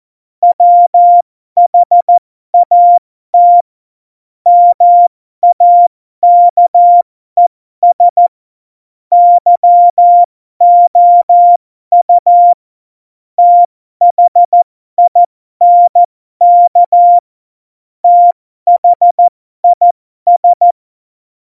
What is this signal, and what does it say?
Morse code, heard as a keyed tone, about 14 wpm, spaced as for 12 wpm, 700 Hz. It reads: WHAT MAKES YOU THINK THIS
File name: Morsekurs / Witze